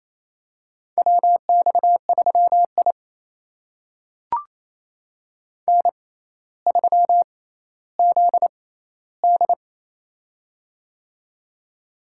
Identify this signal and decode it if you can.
Morse code, heard as a keyed tone, about 28 wpm, spaced as for 14 wpm, 700 Hz. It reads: WX3I N3ZD
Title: USA Call Signs - 4 Characters - 6x Character Spacing - 28wpm